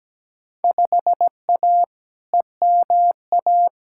Morse code, heard as a keyed tone, about 17 wpm, 700 Hz. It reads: 5A EMA
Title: Field Day Exchanges - 17wpm